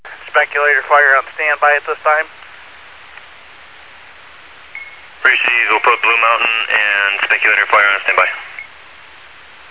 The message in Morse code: none

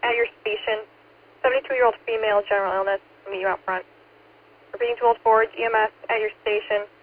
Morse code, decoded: none

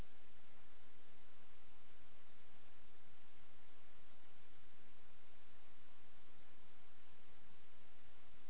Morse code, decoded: none

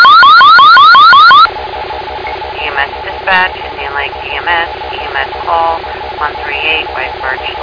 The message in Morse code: none

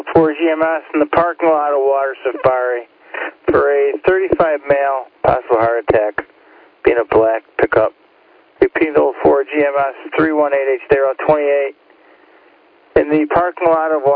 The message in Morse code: none

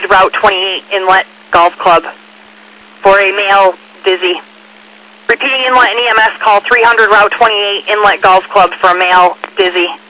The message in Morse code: none

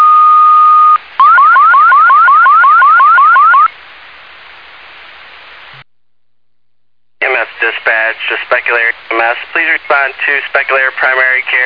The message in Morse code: none

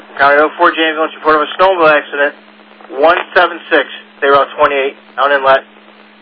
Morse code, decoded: none

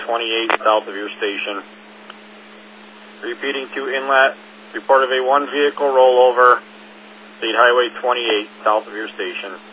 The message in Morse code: none

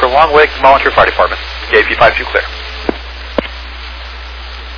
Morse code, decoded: none

